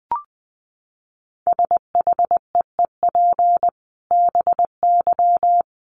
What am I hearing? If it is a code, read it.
Morse code, heard as a keyed tone, about 20 wpm, 700 Hz. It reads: SHEEP BY
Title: Dit Challenge 20wpm